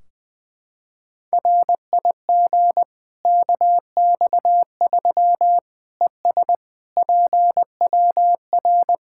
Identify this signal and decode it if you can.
Morse code, heard as a keyed tone, about 20 wpm, 700 Hz. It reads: RIG KX3 ES PWR
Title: K1USN SST QSO Practice 20wpm